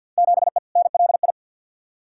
Morse code, heard as a keyed tone, about 50 wpm, 700 Hz. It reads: 7E NLI
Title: Field Day Exchanges Encore - 50wpm